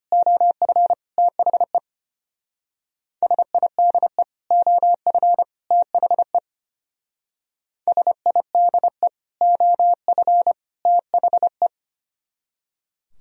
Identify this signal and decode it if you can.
Morse code, heard as a keyed tone, about 31 wpm, 700 Hz. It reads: OF THE SIDE OF THE SIDE OF THE
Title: Speed-Racing - Sets of 3 Words 25wpm